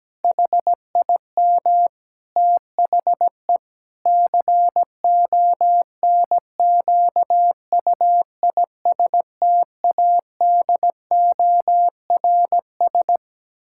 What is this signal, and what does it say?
Morse code, heard as a keyed tone, about 17 wpm, 700 Hz. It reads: HIM THE CONQUISTADORS
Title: Pangram Challenge v2 - 17wpm